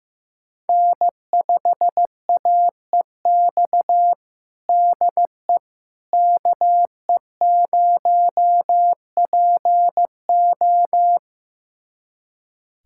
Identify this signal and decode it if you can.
Morse code, heard as a keyed tone, about 15 wpm, 700 Hz. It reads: N5AEX DE KE0PO
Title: USA Callsign DE Callsign - 5 Characters - 15wpm